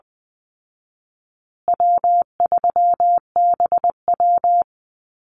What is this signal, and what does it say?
Morse code, heard as a keyed tone, about 20 wpm, 700 Hz. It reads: W3BW
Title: Easy Call Signs Repeated 3x - 20wpm